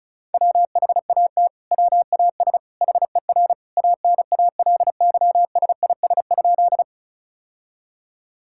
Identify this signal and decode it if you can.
Morse code, heard as a keyed tone, about 36 wpm, 700 Hz. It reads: WHAT WAS HER ANALYSIS?
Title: Sentences from Top 1000 Words - Review Part 5 - 35wpm